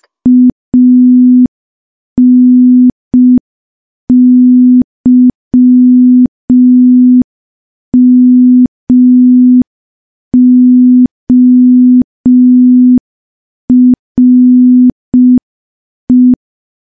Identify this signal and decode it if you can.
Morse code, heard as a keyed tone, about 5 wpm, 250 Hz. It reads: ANYMORE
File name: morse / morse-training-05WPM